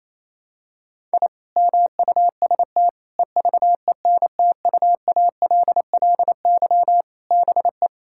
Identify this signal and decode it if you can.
Morse code, heard as a keyed tone, about 28 wpm, 700 Hz. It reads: I MUST EVENTUALLY BE
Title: Sentences from Top 1000 Words - Review Part 6 - 28wpm